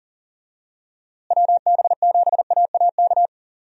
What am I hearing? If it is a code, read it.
Morse code, heard as a keyed tone, about 40 wpm, 700 Hz. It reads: WB7AAK
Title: USA Call Signs - 6 Characters - Repeated 2x - 40wpm